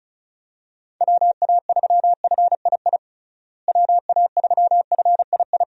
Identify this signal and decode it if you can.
Morse code, heard as a keyed tone, about 36 wpm, 700 Hz. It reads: WA3FII WA3FII
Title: USA Call Signs - 6 Characters - Repeated 2x - 35wpm